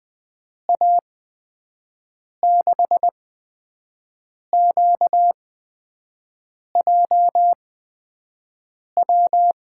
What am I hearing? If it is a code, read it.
Morse code, heard as a keyed tone, about 20 wpm, spaced as for 5 wpm, 700 Hz. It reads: A6QJW
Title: USA Call Signs - 6 Characters - 8x Character Spacing - 20wpm